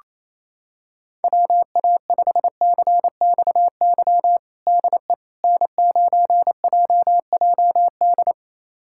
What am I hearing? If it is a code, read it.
Morse code, heard as a keyed tone, about 28 wpm, 700 Hz. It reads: WA5CXY DE N9JJD
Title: USA Callsign DE Callsign 28wpm